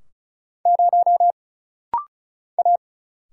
Morse code, heard as a keyed tone, about 35 wpm, 700 Hz. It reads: 0 A
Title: Single Letter-Number Alt - Rapid-Fire 35wpm